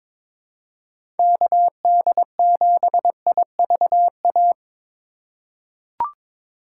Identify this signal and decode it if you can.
Morse code, heard as a keyed tone, about 23 wpm, 700 Hz. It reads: KD7IVA
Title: USA Call Signs - 2x Character Spacing - 22wpm